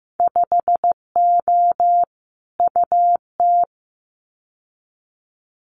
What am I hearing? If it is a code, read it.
Morse code, heard as a keyed tone, about 15 wpm, 700 Hz. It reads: 5O UT
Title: Winter Field Day Exchanges 15wpm